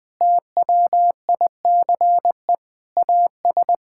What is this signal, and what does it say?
Morse code, heard as a keyed tone, about 21 wpm, 700 Hz. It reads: TWICE AS
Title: Sentences from Top 1600 Words 20wpm